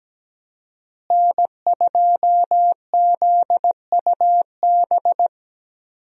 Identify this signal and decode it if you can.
Morse code, heard as a keyed tone, about 17 wpm, 700 Hz. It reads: N2ZUB
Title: USA Call Signs - 5 Characters - Repeated 3x - 17wpm